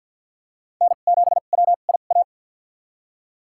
Morse code, heard as a keyed tone, about 51 wpm, spaced as for 36 wpm, 690 Hz. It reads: N7WIA
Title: USA Call Signs - 5 Characters - 2x Character Spacing - 50wpm